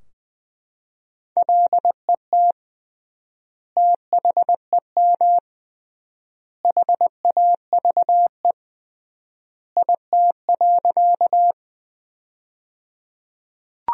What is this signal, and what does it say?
Morse code, heard as a keyed tone, about 21 wpm, 700 Hz. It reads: LET THEM HAVE IT.
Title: 3x Word-Spacing - Sentences from Top 300 Words - 20wpm